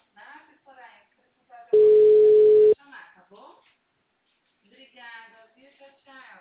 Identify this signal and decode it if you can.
Ringback tone (Europe, 425 Hz), with speech.